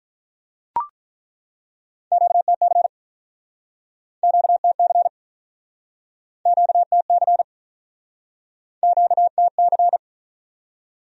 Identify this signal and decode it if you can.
Morse code, heard as a keyed tone, about 49 wpm, 700 Hz. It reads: QTC QTC QTC QTC